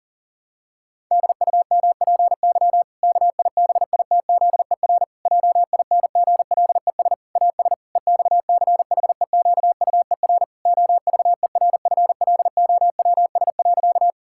Morse code, heard as a keyed tone, about 37 wpm, 700 Hz. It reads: DUMPY KIBITZER JINGLES AS EXCHEQUER OVERFLOWS.
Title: Pangram Challenge v2 - 40wpm